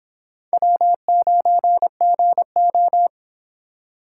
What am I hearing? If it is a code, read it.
Morse code, heard as a keyed tone, about 26 wpm, 700 Hz. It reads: W9GO